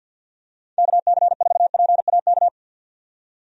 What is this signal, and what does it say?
Morse code, heard as a keyed tone, about 48 wpm, 700 Hz. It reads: KC4PAK
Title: USA Call Signs - 6 Characters - 50wpm